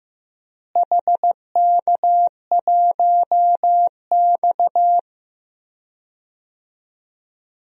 Morse code, heard as a keyed tone, about 15 wpm, 700 Hz. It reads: HK1X
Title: DXCC-100 Call Signs - 4 Characters - Repeated 2x - 15wpm